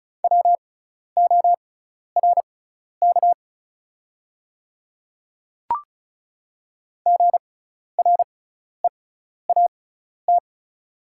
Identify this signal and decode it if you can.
Morse code, heard as a keyed tone, about 34 wpm, spaced as for 12 wpm, 700 Hz. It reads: WORK GREAT
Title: 6x Character Spacing - Top 200 Words - 35wpm